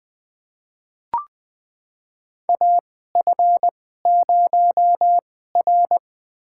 Morse code, heard as a keyed tone, about 21 wpm, spaced as for 15 wpm, 700 Hz. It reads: AF0R
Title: USA Call Signs - 4 Characters - 2x Character Spacing - 20wpm